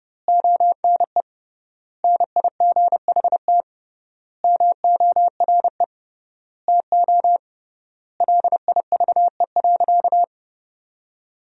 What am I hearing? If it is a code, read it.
Morse code, heard as a keyed tone, about 30 wpm, 700 Hz. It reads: ONE NIGHT MORE TO LIVE.
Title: 3x Word-Spacing - Sentences from Top 300 Words - 30wpm